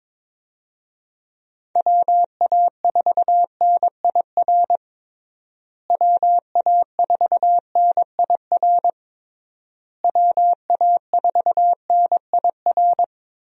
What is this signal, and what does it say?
Morse code, heard as a keyed tone, about 21 wpm, 700 Hz. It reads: WA4NIR WA4NIR WA4NIR
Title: USA Call Signs - 6 Characters - Repeated 3x - 22wpm